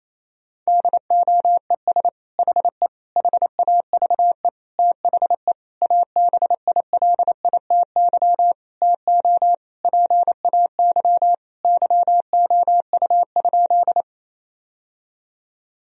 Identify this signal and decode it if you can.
Morse code, heard as a keyed tone, about 28 wpm, 700 Hz. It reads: DOES HE HAVE THE ABILITY TO PAY YOU?
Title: Sentences from Top 1000 Words - Review Part 3 - 28wpm